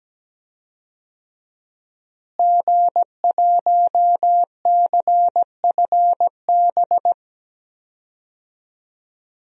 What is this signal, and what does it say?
Morse code, heard as a keyed tone, about 17 wpm, 700 Hz. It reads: G1CFB
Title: DXCC-100 Call Signs - 5 Characters - Repeated 2x - 17wpm